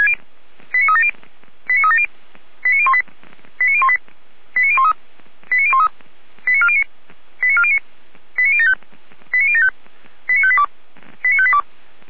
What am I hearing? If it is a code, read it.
Five-tone selective call (ZVEI), heard as quick runs of five tones, tones of about 70 ms. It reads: 89370 89370 89018 89018 89013 89013 89409 89409 89075 89075 89582 89582
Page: feuerwehr-orlamuende